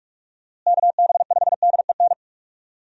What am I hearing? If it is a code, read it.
Morse code, heard as a keyed tone, about 41 wpm, 700 Hz. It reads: KB5DEN
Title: USA Call Signs - 6 Characters - Repeated 2x - 45wpm